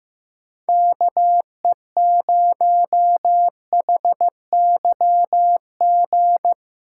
Morse code, heard as a keyed tone, about 15 wpm, 700 Hz. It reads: KE0HYG
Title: USA Callsign DE Callsign - 6 Characters - 15wpm